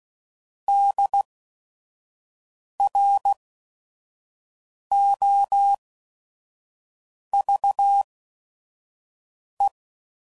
Morse code, heard as a keyed tone, about 16 wpm, spaced as for 5 wpm, 800 Hz. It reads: DROVE
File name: Prefixes / By First Letter / Ds